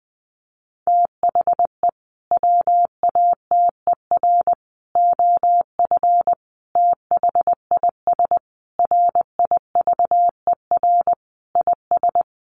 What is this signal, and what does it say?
Morse code, heard as a keyed tone, about 20 wpm, 700 Hz. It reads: THE WATER OF THIS RIVER IS
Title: Sentences from Top 1500 Words 20wpm